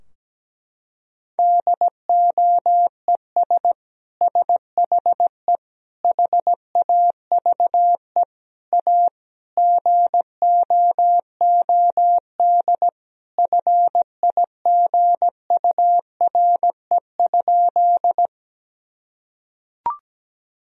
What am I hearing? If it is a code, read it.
Morse code, heard as a keyed tone, about 17 wpm, 700 Hz. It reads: DOES SHE HAVE A GOOD FIGURE?
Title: Sentences from Top 1000 Words - Review Part 2 - 17wpm